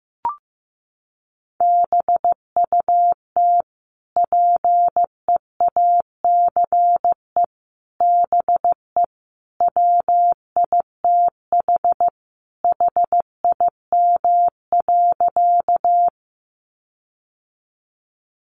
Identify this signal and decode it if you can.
Morse code, heard as a keyed tone, about 15 wpm, 700 Hz. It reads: BUT PEACE BE WITH HIM.
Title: Sentences from Top 1500 Words 15wpm